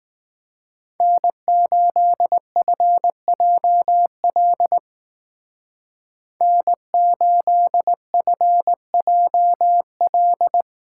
Morse code, heard as a keyed tone, about 18 wpm, 700 Hz. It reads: N8FJL N8FJL
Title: Speed Racing - USA 1x3 Call Signs 15wpm